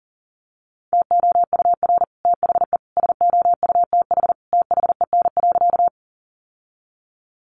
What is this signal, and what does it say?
Morse code, heard as a keyed tone, about 40 wpm, 700 Hz. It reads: TOUR THE SOUTH THEN.
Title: Sentences from Top 1400 Words 40wpm